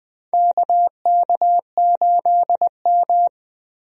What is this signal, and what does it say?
Morse code, heard as a keyed tone, about 20 wpm, 700 Hz. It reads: KK8M
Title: USA Call Signs - 4 Characters - Repeated 3x - 20wpm